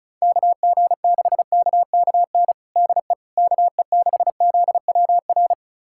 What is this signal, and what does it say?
Morse code, heard as a keyed tone, about 35 wpm, 700 Hz. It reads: KG6KKN DE KE6ZWR
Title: USA Callsign DE Callsign - 6 Characters - 35wpm